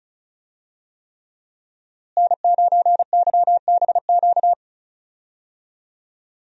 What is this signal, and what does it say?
Morse code, heard as a keyed tone, about 36 wpm, 700 Hz. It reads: N9YBQ
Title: USA Call Signs - 5 Characters - Repeated 3x - 35wpm